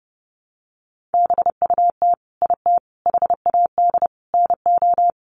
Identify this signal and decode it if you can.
Morse code, heard as a keyed tone, about 30 wpm, 700 Hz. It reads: BUT IT HAD NO
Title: Sentences from Top 1000 Words 30wpm